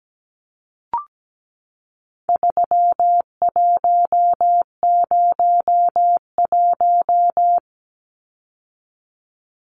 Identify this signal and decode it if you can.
Morse code, heard as a keyed tone, about 17 wpm, 700 Hz. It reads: 3101
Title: Numbers - 4-Digit 17wpm